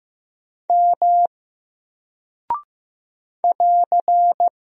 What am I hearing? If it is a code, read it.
Morse code, heard as a keyed tone, about 15 wpm, 700 Hz. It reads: M <AR>
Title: Single Character (Letter:Number:Punctuation:Prosign) 15wpm